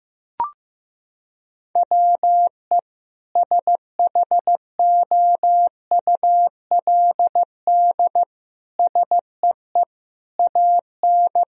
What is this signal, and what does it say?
Morse code, heard as a keyed tone, about 16 wpm, 700 Hz. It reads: WE SHOULD SEE AN
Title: Sentences from Top 1000 Words - Review Part 4 - 15wpm